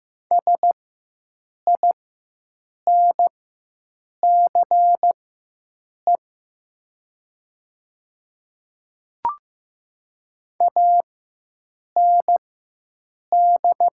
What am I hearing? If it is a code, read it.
Morse code, heard as a keyed tone, about 15 wpm, spaced as for 7 wpm, 700 Hz. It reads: SINCE AND
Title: 4x Character Spacing - Top 300 Words - 15wpm